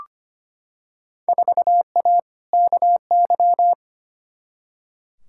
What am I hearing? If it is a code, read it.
Morse code, heard as a keyed tone, about 25 wpm, 700 Hz. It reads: 4A KY
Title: Field Day Exchanges Encore - 25wpm